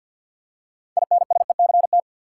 Morse code, heard as a keyed tone, about 46 wpm, 700 Hz. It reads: INSECT
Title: Six Letter Words - 50wpm